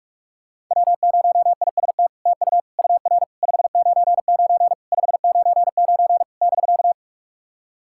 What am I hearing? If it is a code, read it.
Morse code, heard as a keyed tone, about 44 wpm, 700 Hz. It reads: W0IST TU UR 599 599 <BK>
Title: SOTA QSO Practice 45wpm